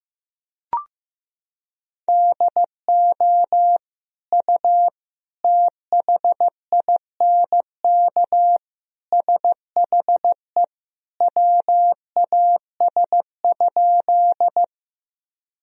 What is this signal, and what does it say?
Morse code, heard as a keyed tone, about 16 wpm, 700 Hz. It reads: DO U THINK SHE WAS?
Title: Sentences from Top 200 Words and Abbreviations 15wpm